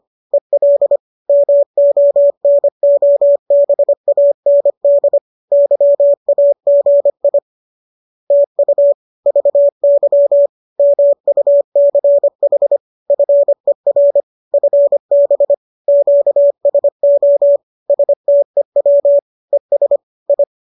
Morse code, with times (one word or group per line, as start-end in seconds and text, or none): 0.3-1.0 EL
1.3-5.2 MONOBAND
5.5-7.4 YAGI
8.3-8.9 TU
9.3-10.5 VY
10.8-12.8 MUCH
13.1-14.2 FER
14.5-15.5 FB
15.9-17.6 QSO
17.9-19.2 STEW
19.5-20.0 ES
20.3-20.4 I